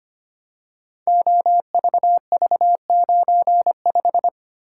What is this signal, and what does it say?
Morse code, heard as a keyed tone, about 25 wpm, 700 Hz. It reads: OVV95
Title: Code Groups - 5 Letter:Numbers - 25wpm